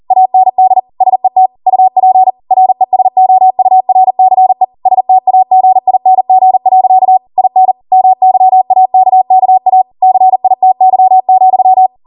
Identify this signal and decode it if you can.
Morse code, heard as a keyed tone, about 40 wpm, 750 Hz. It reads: AND SET UP RESOURCE STAGING. IN MYAKKA CITY,